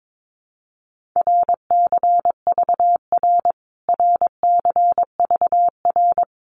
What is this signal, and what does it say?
Morse code, heard as a keyed tone, about 22 wpm, 700 Hz. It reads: RCVR RCVR
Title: Abbreviations Repeated 3x - 22wpm